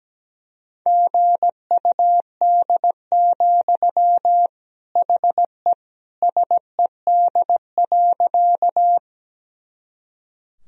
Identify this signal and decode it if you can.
Morse code, heard as a keyed tone, about 17 wpm, 700 Hz. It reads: GUD, HE SED.
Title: Sentences from Top 200 Words and Abbreviations 17wpm